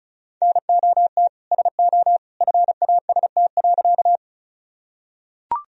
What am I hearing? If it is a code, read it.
Morse code, heard as a keyed tone, about 34 wpm, 700 Hz. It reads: NOT SO FAST.
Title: Sentences from Top 1000 Words - Review Part 2 - 35wpm